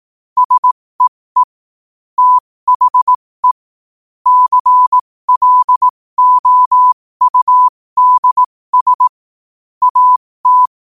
Morse code, heard as a keyed tone, about 18 wpm, spaced as for 15 wpm, 1000 Hz. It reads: SEE THE CLOUDS AT
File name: qotdfiles / 15wpm